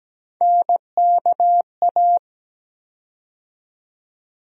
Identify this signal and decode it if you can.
Morse code, heard as a keyed tone, about 17 wpm, 700 Hz. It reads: NKA
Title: USA Call Sign Postfixes - 3 Letters 17wpm